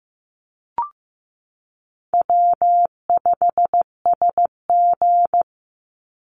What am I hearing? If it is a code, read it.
Morse code, heard as a keyed tone, about 15 wpm, 700 Hz. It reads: W5SG
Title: Call Signs Repeated 3x - 15wpm